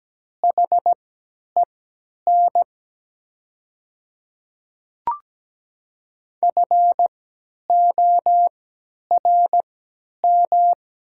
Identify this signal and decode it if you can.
Morse code, heard as a keyed tone, about 17 wpm, spaced as for 10 wpm, 690 Hz. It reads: HEN FORM
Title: 3x Character Spacing - Top 300 Words - 17wpm